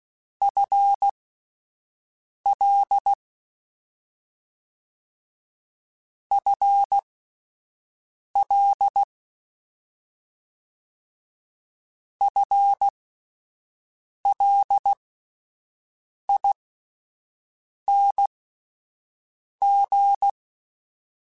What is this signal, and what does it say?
Morse code, heard as a keyed tone, about 16 wpm, spaced as for 5 wpm, 800 Hz. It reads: FL FL FLING